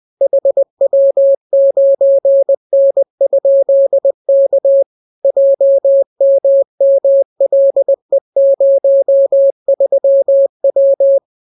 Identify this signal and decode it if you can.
Morse code, heard as a keyed tone, about 20 wpm, 550 Hz. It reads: HW9N?K JMMLE03W